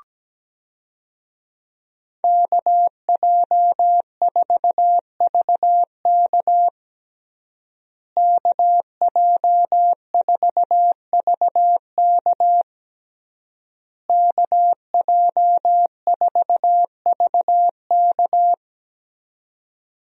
Morse code, heard as a keyed tone, about 17 wpm, 700 Hz. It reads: KJ4VK KJ4VK KJ4VK
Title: USA Call Signs - 5 Characters - Repeated 3x - 17wpm